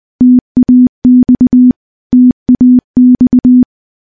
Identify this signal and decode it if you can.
Morse code, heard as a keyed tone, about 21 wpm, 250 Hz. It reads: TAX TAX
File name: morse-training-20WPM